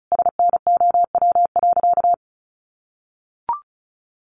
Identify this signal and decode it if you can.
Morse code, heard as a keyed tone, about 34 wpm, 700 Hz. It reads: SNOW.
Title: Easy Phrases 35wpm